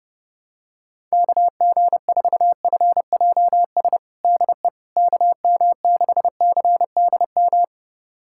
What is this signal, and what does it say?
Morse code, heard as a keyed tone, about 32 wpm, 700 Hz. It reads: KG4FJS DE KM6CDM
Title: USA Callsign DE Callsign 30wpm